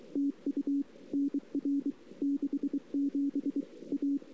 Morse code, heard as a keyed tone, about 23 wpm, 280 Hz. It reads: TU NR 67 A
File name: CW / 2008-ss-cw-k5tr-op-n5rz -23:00